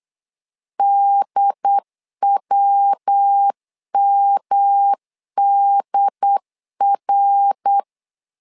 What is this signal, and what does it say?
Morse code, heard as a keyed tone, about 8 wpm, 800 Hz. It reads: DWMDR
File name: LEZIONI / 4 Q 8wpm 2732